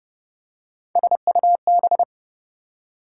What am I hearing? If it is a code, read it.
Morse code, heard as a keyed tone, about 30 wpm, 700 Hz. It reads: SUB
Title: Prefixes and Suffixes 30wpm